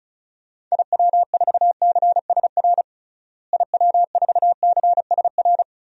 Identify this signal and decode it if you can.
Morse code, heard as a keyed tone, about 34 wpm, 700 Hz. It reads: IW4CSR IW4CSR
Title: DXCC-100 Call Signs - 6 Characters - Repeated 3x - 35wpm